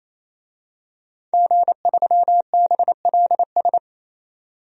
Morse code, heard as a keyed tone, about 28 wpm, 700 Hz. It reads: G3BLS
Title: DXCC-100 Call Signs - 5 Characters - 28wpm